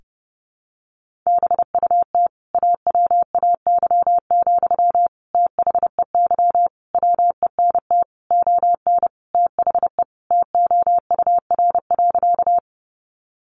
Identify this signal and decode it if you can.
Morse code, heard as a keyed tone, about 30 wpm, 700 Hz. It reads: BUT AWAY, THEY WENT ON THE TOUR.
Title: Sentences from Top 1400 Words 30wpm